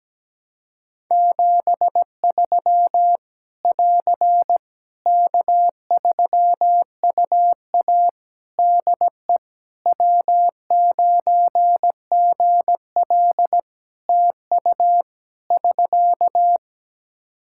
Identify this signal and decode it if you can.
Morse code, heard as a keyed tone, about 17 wpm, 700 Hz. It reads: 73 <AR> K3UA DE W9GL TU <SK>
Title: Practice QSO - Exchange 4 - Ending - 17wpm